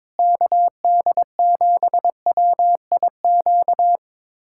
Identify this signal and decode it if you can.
Morse code, heard as a keyed tone, about 22 wpm, 700 Hz. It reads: KD7WIQ
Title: USA Callsign DE Callsign - 6 Characters - 22wpm